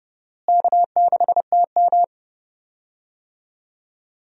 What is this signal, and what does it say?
Morse code, heard as a keyed tone, about 32 wpm, 700 Hz. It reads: K6TM
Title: USA Call Signs - 4 Characters - Repeated 2x - 30wpm